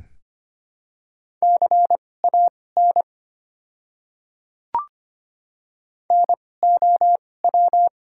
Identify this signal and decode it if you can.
Morse code, heard as a keyed tone, about 25 wpm, spaced as for 18 wpm, 700 Hz. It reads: CAN NOW
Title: 2x Character Spacing - Top 100 Words - 25wpm